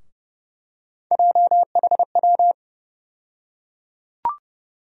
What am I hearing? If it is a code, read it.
Morse code, heard as a keyed tone, about 30 wpm, 700 Hz. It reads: JHW
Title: USA Call Sign Postfixes - 3 Letters 30wpm